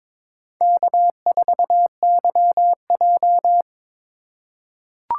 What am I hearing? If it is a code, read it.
Morse code, heard as a keyed tone, about 22 wpm, 700 Hz. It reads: K4YJ
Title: USA Call Signs - 4 Characters - 2x Character Spacing - 22wpm